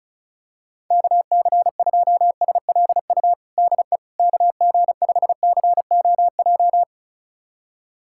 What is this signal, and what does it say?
Morse code, heard as a keyed tone, about 36 wpm, 700 Hz. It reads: KC2SLU DE KG5COJ
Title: USA Callsign DE Callsign - 6 Characters - 35wpm